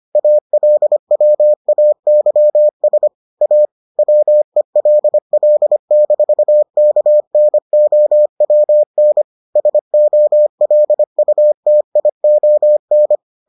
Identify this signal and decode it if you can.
Morse code, heard as a keyed tone, about 25 wpm, 600 Hz. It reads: ALWAYS A WELL-KNOWN SOLUTION